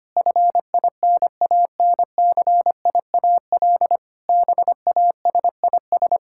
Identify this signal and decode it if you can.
Morse code, heard as a keyed tone, about 25 wpm, 700 Hz. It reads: FINANCIAL BASIS
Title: Sentences from Top 1000 Words - Review Part 5 - 25wpm